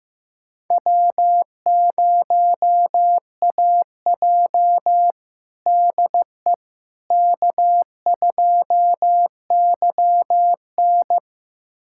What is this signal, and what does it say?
Morse code, heard as a keyed tone, about 16 wpm, 700 Hz. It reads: W0AJ DE K2YN
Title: USA Callsign DE Callsign - 4 Characters - 15wpm